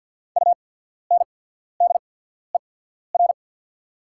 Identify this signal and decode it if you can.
Morse code, heard as a keyed tone, about 49 wpm, spaced as for 14 wpm, 690 Hz. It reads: UNDER
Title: 8x Character Spacing - Top 300 Words - 50wpm